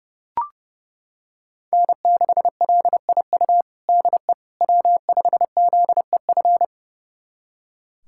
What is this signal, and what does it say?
Morse code, heard as a keyed tone, about 32 wpm, 700 Hz. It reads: N6LIU DE W5ZEF
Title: USA Callsign DE Callsign - 5 Characters - 30wpm